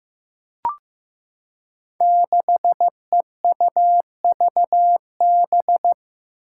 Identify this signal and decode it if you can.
Morse code, heard as a keyed tone, about 15 wpm, 700 Hz. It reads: 6EUVB
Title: Five Challenging Letter-Number Combinations 15wpm